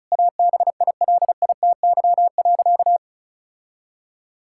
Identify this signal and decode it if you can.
Morse code, heard as a keyed tone, about 36 wpm, 700 Hz. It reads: ABILITY.
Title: Sentences from Top 1000 Words - Review Part 6 - 35wpm